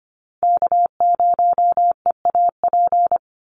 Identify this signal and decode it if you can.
Morse code, heard as a keyed tone, about 25 wpm, 700 Hz. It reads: K0EAP
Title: NA SOTA Chaser Callsigns 25wpm